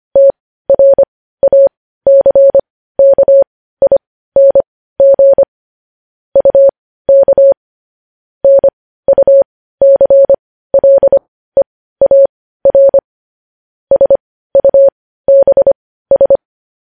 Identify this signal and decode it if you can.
Morse code, heard as a keyed tone, about 25 wpm, spaced as for 15 wpm, 550 Hz. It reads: TRACKING UK NUCLEAR SUBS
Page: News Headlines in Morse Code at 15 WPM